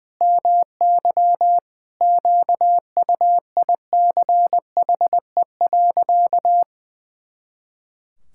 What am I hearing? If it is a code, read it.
Morse code, heard as a keyed tone, about 20 wpm, 700 Hz. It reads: MY QUICHE.
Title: Pangram Challenge v2 - 20wpm